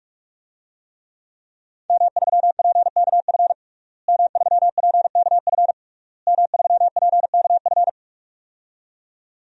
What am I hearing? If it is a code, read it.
Morse code, heard as a keyed tone, about 46 wpm, 700 Hz. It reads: M3PKF M3PKF M3PKF
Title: DXCC-100 Call Signs - 5 Characters - Repeated 3x - 45wpm